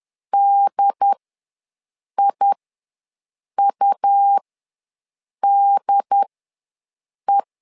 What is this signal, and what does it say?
Morse code, heard as a keyed tone, about 10 wpm, spaced as for 6 wpm, 800 Hz. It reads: DIUDE
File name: LEZIONI / 2 UHDN / 2 UHD ws 10wpm 2058